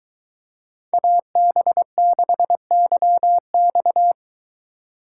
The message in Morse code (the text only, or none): AB6YX